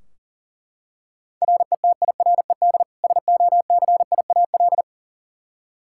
Morse code, heard as a keyed tone, about 40 wpm, 700 Hz. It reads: RETIRED SOCIAL
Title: K1USN SST QSO Practice 40wpm